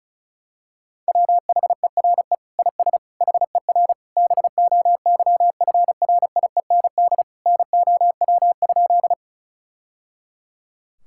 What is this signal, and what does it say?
Morse code, heard as a keyed tone, about 36 wpm, 700 Hz. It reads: WHERE IS HER BOYFRIEND NOW?
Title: Sentences from Top 1000 Words - Review Part 5 - 35wpm